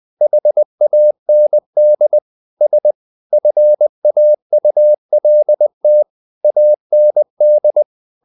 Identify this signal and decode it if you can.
Morse code, heard as a keyed tone, about 20 wpm, 600 Hz. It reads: HAND S FAULT AND
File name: the old man and the sea / the old man and the sea part 1 20wpm